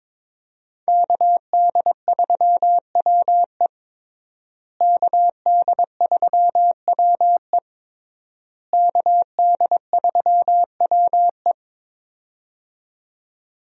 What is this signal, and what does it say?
Morse code, heard as a keyed tone, about 22 wpm, 700 Hz. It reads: KD3WE KD3WE KD3WE
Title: USA Call Signs - 5 Characters - Repeated 3x - 22wpm